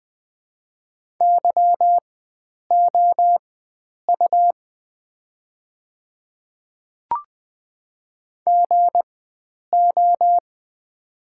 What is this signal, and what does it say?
Morse code, heard as a keyed tone, about 20 wpm, spaced as for 9 wpm, 700 Hz. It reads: YOU GO
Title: 4x Character Spacing - Top 100 Words - 20wpm